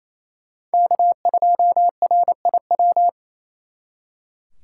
Morse code, heard as a keyed tone, about 36 wpm, 700 Hz. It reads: K2RIW